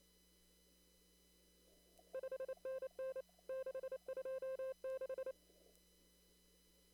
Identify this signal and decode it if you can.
Morse code, heard as a keyed tone, about 26 wpm, 530 Hz. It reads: 5NN 626
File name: CW / kt5j wpx cw-op-k5pi